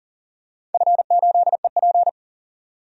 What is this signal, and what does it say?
Morse code, heard as a keyed tone, about 38 wpm, 700 Hz. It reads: F8EP
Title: DXCC-100 Call Signs - 4 Characters - 40wpm